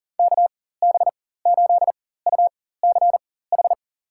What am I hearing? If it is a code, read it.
Morse code, heard as a keyed tone, about 40 wpm, spaced as for 19 wpm, 700 Hz. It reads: KB8UCH
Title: USA Call Signs - 6 Characters - 4x Character Spacing - 40wpm